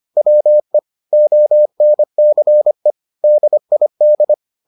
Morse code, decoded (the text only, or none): WE ONCE DID